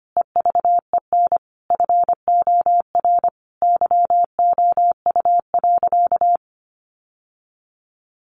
Morse code, heard as a keyed tone, about 25 wpm, 700 Hz. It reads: EVEN FOR YOU.
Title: Sentences from Top 700 Words 25wpm